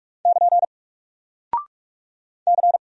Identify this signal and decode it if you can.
Morse code, heard as a keyed tone, about 44 wpm, 700 Hz. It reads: <KN> C
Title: Single Character (Letter:Number:Punctuation:Prosign) 45wpm